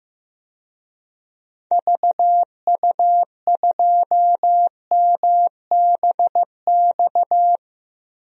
Morse code, heard as a keyed tone, about 15 wpm, 700 Hz. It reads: VU2MBX
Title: DXCC-100 Call Signs - 6 Characters - 15wpm